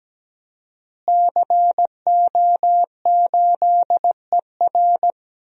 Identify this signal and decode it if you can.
Morse code, heard as a keyed tone, about 17 wpm, 700 Hz. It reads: CO8ER
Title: DXCC-100 Call Signs - 5 Characters - Repeated 2x - 17wpm